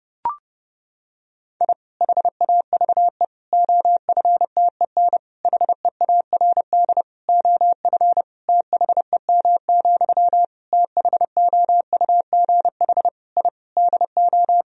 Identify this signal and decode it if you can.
Morse code, heard as a keyed tone, about 30 wpm, 700 Hz. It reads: I HAVE OFTEN HEARD OF THEM, THOUGH I DO
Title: Sentences from Top 1000 Words - Review Part 2 - 30wpm